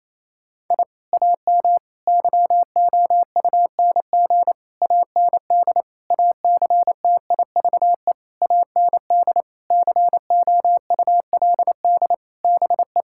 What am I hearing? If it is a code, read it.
Morse code, heard as a keyed tone, about 28 wpm, 700 Hz. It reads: I AM YOUNG AND ACTIVE AND COULD BE
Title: Sentences from Top 1000 Words - Review Part 6 - 28wpm